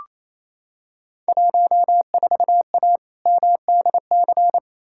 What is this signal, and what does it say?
Morse code, heard as a keyed tone, about 28 wpm, 700 Hz. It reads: 14A MDC
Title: Field Day Exchanges Encore - 28wpm